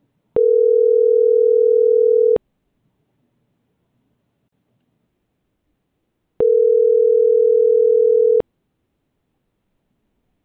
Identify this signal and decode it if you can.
Ringback tone (North America), 440 + 480 Hz.